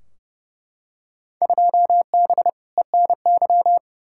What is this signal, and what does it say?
Morse code, heard as a keyed tone, about 31 wpm, 700 Hz. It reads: 2B ENY
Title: Field Day Exchanges Encore - 30wpm